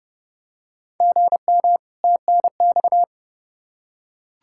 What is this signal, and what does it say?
Morse code, heard as a keyed tone, about 30 wpm, 700 Hz. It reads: GM TNX
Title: Practice QSO - Exchange 1 - Beginning - 30wpm